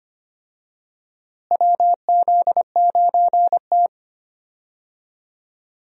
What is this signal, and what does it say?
Morse code, heard as a keyed tone, about 25 wpm, 700 Hz. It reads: WZ9T
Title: USA Call Signs - 4 Characters - Repeated 2x - 25wpm